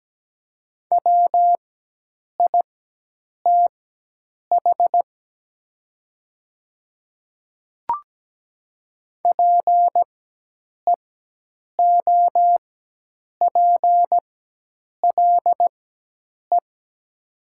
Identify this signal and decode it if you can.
Morse code, heard as a keyed tone, about 17 wpm, spaced as for 8 wpm, 700 Hz. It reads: WITH PEOPLE
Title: 4x Character Spacing - Top 100 Words - 17wpm